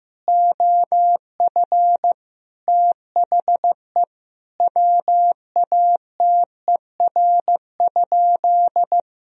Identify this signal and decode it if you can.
Morse code, heard as a keyed tone, about 15 wpm, 700 Hz. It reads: OF THE WATER?
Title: Sentences from Top 1000 Words - Review Part 3 - 15wpm